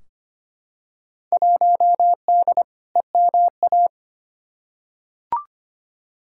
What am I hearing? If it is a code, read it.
Morse code, heard as a keyed tone, about 25 wpm, 700 Hz. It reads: 1D EMA
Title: Field Day Exchanges - 25wpm